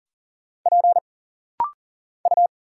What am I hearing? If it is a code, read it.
Morse code, heard as a keyed tone, about 39 wpm, 700 Hz. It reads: P U
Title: Single Letter-Number - Rapid-Fire 40wpm